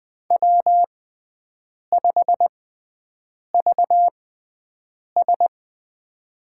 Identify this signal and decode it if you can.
Morse code, heard as a keyed tone, about 21 wpm, spaced as for 7 wpm, 700 Hz. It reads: W5VS
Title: USA Call Signs - 6x Character Spacing - 20wpm